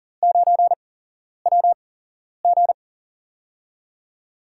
Morse code, heard as a keyed tone, about 42 wpm, spaced as for 11 wpm, 700 Hz. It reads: 9WG
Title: USA Call Signs - 8x Character Spacing - 40wpm